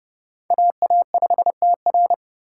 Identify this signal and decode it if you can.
Morse code, heard as a keyed tone, about 30 wpm, 700 Hz. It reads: AA5TR
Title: USA Call Signs - 5 Characters - 3x Character Spacing - 30wpm